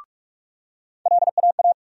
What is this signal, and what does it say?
Morse code, heard as a keyed tone, about 42 wpm, 700 Hz. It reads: LAA
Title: USA Call Sign Postfixes - 3 Letters 45wpm